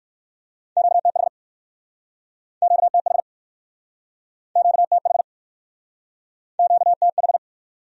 Morse code, heard as a keyed tone, about 63 wpm, 700 Hz. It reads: QTH QTH QTH QTH